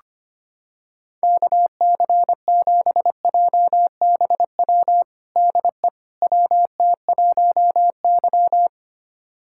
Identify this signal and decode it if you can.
Morse code, heard as a keyed tone, about 25 wpm, 700 Hz. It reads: KC7JBW DE WT1Y
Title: USA Callsign DE Callsign 25wpm